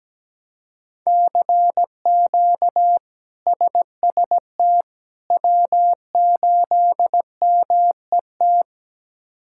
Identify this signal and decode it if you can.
Morse code, heard as a keyed tone, about 17 wpm, 700 Hz. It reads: CQ SST W8MET
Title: K1USN SST QSO Practice - 17wpm